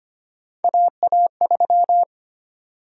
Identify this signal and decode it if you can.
Morse code, heard as a keyed tone, about 25 wpm, 700 Hz. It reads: AA3
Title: USA Call Sign Prefixes - 25wpm